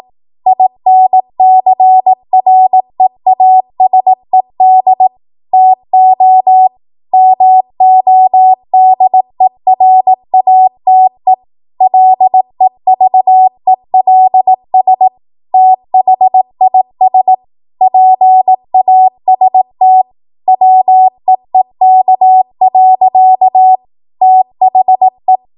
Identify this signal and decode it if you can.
Morse code, heard as a keyed tone, about 18 wpm, 750 Hz. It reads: INCREASED TO MODERATE LEVELS THIS PAST WEEK. THE
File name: Morse / 18WPM